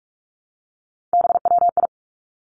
Morse code, heard as a keyed tone, about 46 wpm, 700 Hz. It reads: BWI
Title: Three Challenging Letter-Number Combinations 45wpm